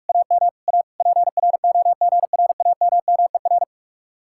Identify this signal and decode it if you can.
Morse code, heard as a keyed tone, about 56 wpm, 700 Hz. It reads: AM A PROGRAMMER